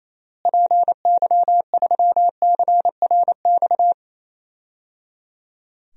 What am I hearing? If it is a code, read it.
Morse code, heard as a keyed tone, about 28 wpm, 700 Hz. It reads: PY3CRX
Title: DXCC-100 Call Signs - 6 Characters - Repeated 3x - 28wpm